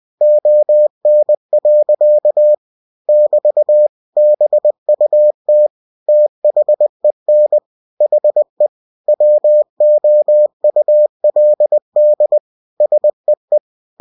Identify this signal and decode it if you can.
Morse code, heard as a keyed tone, about 20 wpm, 600 Hz. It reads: ON. <BT> BUT THEN HE WOULD SEE
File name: cw-books / winnie-the-pooh / winnie-20wpm-page-0036